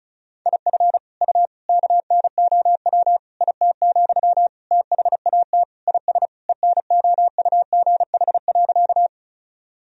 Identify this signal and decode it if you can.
Morse code, heard as a keyed tone, about 36 wpm, 700 Hz. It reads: IF U KNOW IT, THAT IS ENOUGH.
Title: Sentences from Top 400 Words and Abbreviations 35wpm